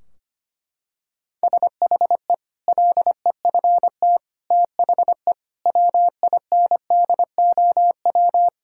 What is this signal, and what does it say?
Morse code, heard as a keyed tone, about 25 wpm, 700 Hz. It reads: SHE LEFT THE WINDOW
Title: Sentences from Top 1600 Words 25wpm